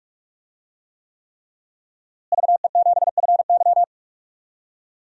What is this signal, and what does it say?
Morse code, heard as a keyed tone, about 46 wpm, 700 Hz. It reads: VE7FY
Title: DXCC-100 Call Signs - 5 Characters - Repeated 3x - 45wpm